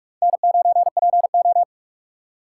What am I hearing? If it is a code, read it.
Morse code, heard as a keyed tone, about 61 wpm, spaced as for 53 wpm, 700 Hz. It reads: N9PO
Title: Speed Racing - USA 1x2 Call Signs 45wpm